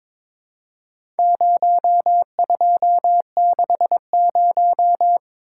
Morse code, heard as a keyed tone, about 22 wpm, 700 Hz. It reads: 0260
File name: practice / 22wpm / Four Letter-Number Combinations - 22wpm